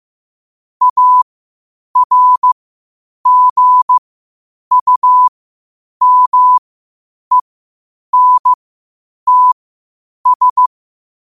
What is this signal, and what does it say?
Morse code, heard as a keyed tone, about 15 wpm, spaced as for 8 wpm, 1000 Hz. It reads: ARGUMENTS
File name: qotdfiles / 8wpm